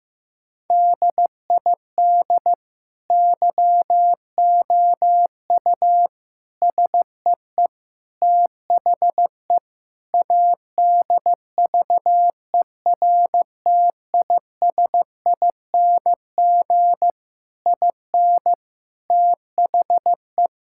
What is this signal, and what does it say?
Morse code, heard as a keyed tone, about 16 wpm, 700 Hz. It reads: DID YOU SEE THE ADVERTISING IN THE
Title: Sentences from Top 1000 Words - Review Part 6 - 15wpm